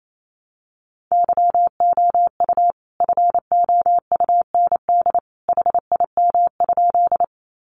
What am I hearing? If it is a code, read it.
Morse code, heard as a keyed tone, about 28 wpm, 700 Hz. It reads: YOU FOUND HIM?
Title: Sentences from Top 300 Words 28wpm